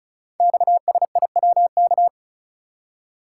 Morse code, heard as a keyed tone, about 36 wpm, 690 Hz. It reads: XSIWK
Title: Code Groups - 5 Letters - 35wpm